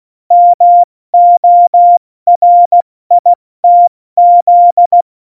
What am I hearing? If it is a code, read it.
Morse code, heard as a keyed tone, about 16 wpm, 700 Hz. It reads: MORITZ
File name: Morsekurs / Witze / Witz 19. Februar 2019, 70 F80